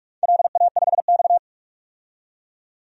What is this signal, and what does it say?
Morse code, heard as a keyed tone, about 46 wpm, 700 Hz. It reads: LA5X
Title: DXCC-100 Call Signs - 4 Characters - 45wpm